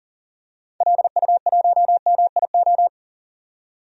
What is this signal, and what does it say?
Morse code, heard as a keyed tone, about 40 wpm, 700 Hz. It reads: LU1MIO
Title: DXCC-100 Call Signs - 6 Characters - 40wpm